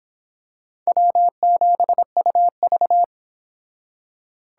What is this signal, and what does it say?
Morse code, heard as a keyed tone, about 26 wpm, 700 Hz. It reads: W7UV